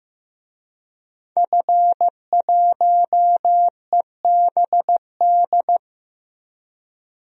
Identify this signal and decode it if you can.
Morse code, heard as a keyed tone, about 15 wpm, 700 Hz. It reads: F1EBD
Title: DXCC-100 Call Signs - 5 Characters - Repeated 3x - 15wpm